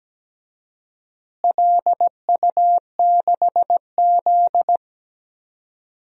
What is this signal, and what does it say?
Morse code, heard as a keyed tone, about 17 wpm, 700 Hz. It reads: LU6Z
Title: DXCC-100 Call Signs - 4 Characters - Repeated 2x - 17wpm